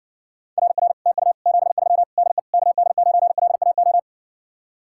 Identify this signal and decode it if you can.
Morse code, heard as a keyed tone, about 61 wpm, 700 Hz. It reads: RR TU 73 DE KD9LAO